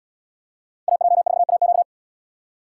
Morse code, heard as a keyed tone, about 77 wpm, 700 Hz. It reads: AJ4AQ